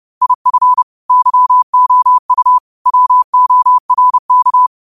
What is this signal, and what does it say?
Morse code, heard as a keyed tone, about 30 wpm, 1000 Hz. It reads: IF YOU WORK
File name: qotdfiles / 30wpm